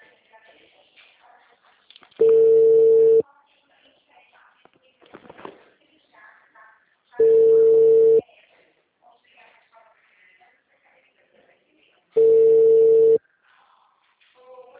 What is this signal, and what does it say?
Ringback tone (China, 450 Hz), with speech.